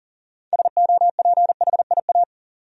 Morse code, heard as a keyed tone, about 39 wpm, 700 Hz. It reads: SOPHIA
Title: USA Common Names - 40wpm